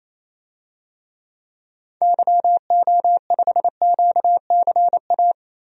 Morse code, heard as a keyed tone, about 28 wpm, 700 Hz. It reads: YO5QCA
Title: DXCC-100 Call Signs - 6 Characters - 28wpm